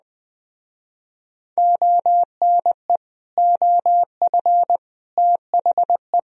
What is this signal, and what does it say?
Morse code, heard as a keyed tone, about 20 wpm, 700 Hz. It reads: ONE OF THE